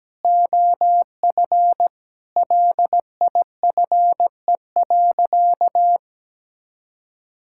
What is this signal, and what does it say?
Morse code, heard as a keyed tone, about 17 wpm, 700 Hz. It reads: OF LIFE.
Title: Sentences from Top 1000 Words - Review Part 6 - 17wpm